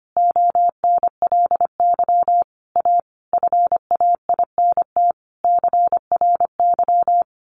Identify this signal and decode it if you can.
Morse code, heard as a keyed tone, about 25 wpm, 700 Hz. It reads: ONLY A FAINT CRY
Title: A Sound Of Thunder 25wpm